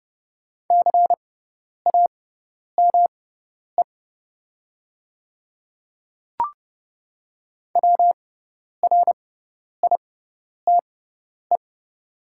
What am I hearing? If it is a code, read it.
Morse code, heard as a keyed tone, about 30 wpm, spaced as for 10 wpm, 700 Hz. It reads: CAME WRITE
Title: 6x Character Spacing - Top 500 Words - 30wpm